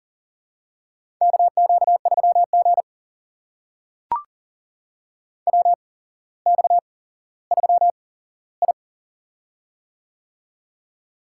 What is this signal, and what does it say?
Morse code, heard as a keyed tone, about 38 wpm, spaced as for 17 wpm, 700 Hz. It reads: KQ3G WX3I
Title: USA Call Signs - 4 Characters - 8x Character Spacing - 40wpm